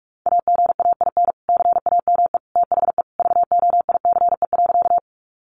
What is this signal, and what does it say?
Morse code, heard as a keyed tone, about 44 wpm, 700 Hz. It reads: AGAIN CAME THE VOICE.
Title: Sentences from Top 500 Words 45wpm